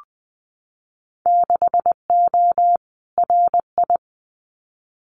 Morse code, heard as a keyed tone, about 19 wpm, 700 Hz. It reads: 6O RI
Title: Winter Field Day Exchanges Encore 20wpm